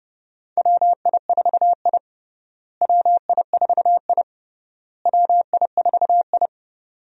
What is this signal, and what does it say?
Morse code, heard as a keyed tone, about 32 wpm, 700 Hz. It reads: WI4I WI4I WI4I
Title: USA Call Signs - 4 Characters - Repeated 3x - 30wpm